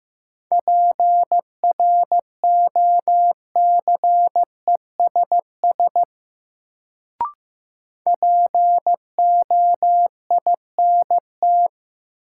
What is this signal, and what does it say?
Morse code, heard as a keyed tone, about 15 wpm, 700 Hz. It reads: PROCESS POINT
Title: Sets of 3 Words - New Word Drill - 15wpm